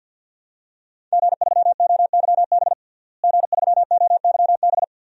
Morse code, heard as a keyed tone, about 52 wpm, 700 Hz. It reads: G3OYB G3OYB
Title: DXCC-100 Call Signs - 5 Characters - Repeated 2x - 50wpm